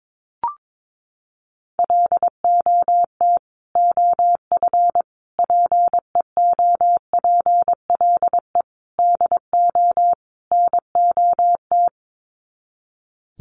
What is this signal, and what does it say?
Morse code, heard as a keyed tone, about 22 wpm, 700 Hz. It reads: LOT OF PEOPLE DO NOT
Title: Sets of 5 Words Encore 22wpm